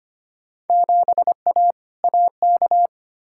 Morse code, heard as a keyed tone, about 25 wpm, 700 Hz. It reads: 7A AK
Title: Field Day Exchanges Encore - 25wpm